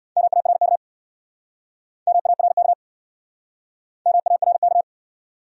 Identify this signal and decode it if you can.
Morse code, heard as a keyed tone, about 68 wpm, 700 Hz. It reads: MARK MARK MARK